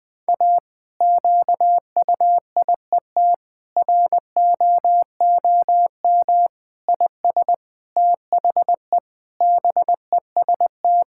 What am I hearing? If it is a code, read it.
Morse code, heard as a keyed tone, about 20 wpm, 700 Hz. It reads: A QUIET ROOM IS THE BEST
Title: Difficult Sentences with Hints - 20wpm